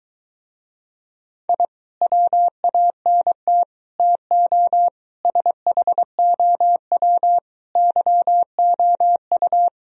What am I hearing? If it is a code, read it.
Morse code, heard as a keyed tone, about 22 wpm, 700 Hz. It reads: I WANT TO SHOW YOU